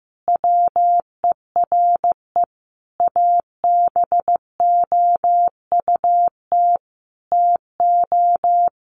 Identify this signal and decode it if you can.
Morse code, heard as a keyed tone, about 15 wpm, 700 Hz. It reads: WERE ABOUT TO
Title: Sentences from Top 1200 Words 15wpm